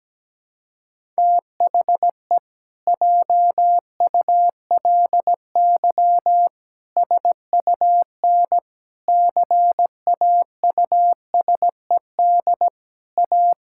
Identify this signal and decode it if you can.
Morse code, heard as a keyed tone, about 17 wpm, 700 Hz. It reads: THE JULY SUN CAUSED A
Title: Pangram Challenge v2 - 17wpm